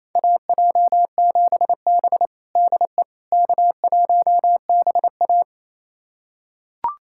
Morse code, heard as a keyed tone, about 28 wpm, 700 Hz. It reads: AJ7B DE K1BA
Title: USA Callsign DE Callsign - 4 Characters - 28wpm